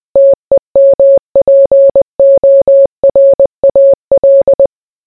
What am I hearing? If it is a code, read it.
Morse code, heard as a keyed tone, about 20 wpm, 550 Hz. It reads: TEMPORAL